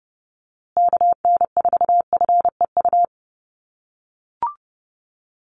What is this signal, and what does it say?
Morse code, heard as a keyed tone, about 30 wpm, 700 Hz. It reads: KN4FEU
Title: USA Call Signs Repeated 2x 30wpm